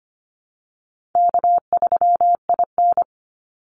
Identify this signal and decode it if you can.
Morse code, heard as a keyed tone, about 25 wpm, 700 Hz. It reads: K3IN
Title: Call Signs - 25wpm